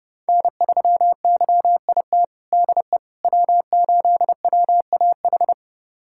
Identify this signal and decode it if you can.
Morse code, heard as a keyed tone, about 30 wpm, 700 Hz. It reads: N3YIT DE W8WAH
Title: USA Callsign DE Callsign - 5 Characters - 30wpm